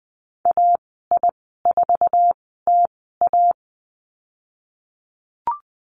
Morse code, heard as a keyed tone, about 20 wpm, spaced as for 15 wpm, 700 Hz. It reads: AI4TA
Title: 055 Lesson 10g - Letter r - Call Signs 20wpm